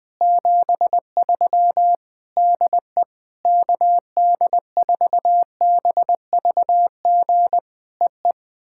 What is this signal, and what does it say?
Morse code, heard as a keyed tone, about 20 wpm, 700 Hz. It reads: 73 DE KD4BVG EE
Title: POTA QSO Practice - 20wpm